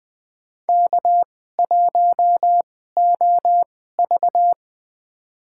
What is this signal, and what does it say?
Morse code, heard as a keyed tone, about 20 wpm, spaced as for 14 wpm, 700 Hz. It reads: K1OV
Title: USA Call Signs - 4 Characters - 2x Character Spacing - 20wpm